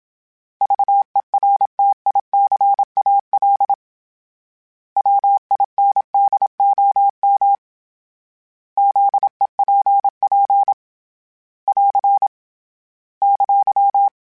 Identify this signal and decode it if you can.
Morse code, heard as a keyed tone, about 27 wpm, 800 Hz. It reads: VERTICAL WINDOM ZEPP <AR> !
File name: fromSMs / FromSMs mp3s / Antennas@25